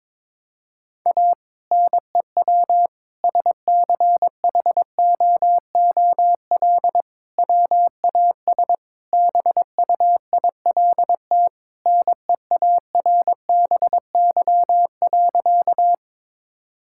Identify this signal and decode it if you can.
Morse code, heard as a keyed tone, about 21 wpm, 700 Hz. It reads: A NEW SCHOOL WAS BUILT NEARBY.
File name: practice / 22wpm / Sentences with 2-Word Phrases - 22wpm